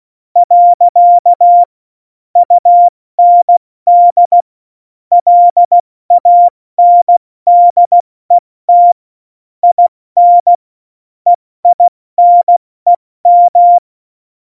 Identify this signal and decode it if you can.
Morse code, heard as a keyed tone, about 16 wpm, 700 Hz. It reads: . UND LANDET IN EINEM
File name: Morsekurs / Witze / Witz 17. November 2020, 70 F80